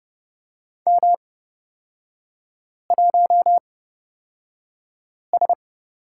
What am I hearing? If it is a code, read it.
Morse code, heard as a keyed tone, about 29 wpm, spaced as for 5 wpm, 690 Hz. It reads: M1S
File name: course / 30:10wpm / 236 Lesson 38e - Letter q - Three Characters 30wpm